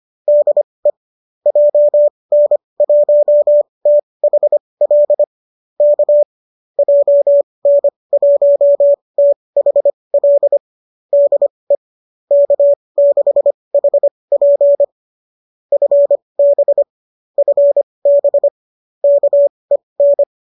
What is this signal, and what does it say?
Morse code, heard as a keyed tone, about 25 wpm, spaced as for 20 wpm, 570 Hz. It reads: DE JN1THL K JN1THL DE K6HP FB FB KEN
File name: qso205 20